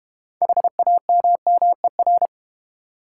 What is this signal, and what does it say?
Morse code, heard as a keyed tone, about 30 wpm, 700 Hz. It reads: HAMMER
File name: Easy Words - Speed-Racing with Hints - 25wpm